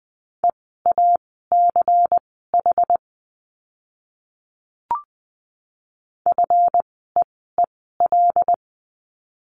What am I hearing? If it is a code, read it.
Morse code, heard as a keyed tone, about 20 wpm, spaced as for 14 wpm, 700 Hz. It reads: EACH FEEL
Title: 192 Lesson 31f - Letter v - Words 20wpm